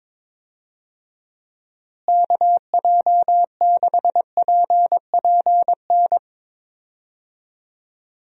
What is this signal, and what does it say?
Morse code, heard as a keyed tone, about 22 wpm, 700 Hz. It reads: KJ6PPN
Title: USA Call Signs - 6 Characters - Repeated 2x - 22wpm